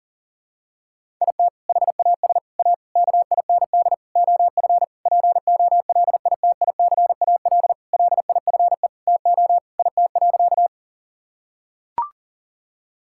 Morse code, heard as a keyed tone, about 37 wpm, 700 Hz. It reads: IT HAS A KIND OF POLITICAL LIFE TO IT.
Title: Sentences from Top 1000 Words - Review Part 5 - 40wpm